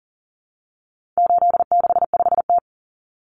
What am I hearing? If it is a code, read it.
Morse code, heard as a keyed tone, about 40 wpm, 700 Hz. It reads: 865T
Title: Common Cut Numbers - 4-Digit 40wpm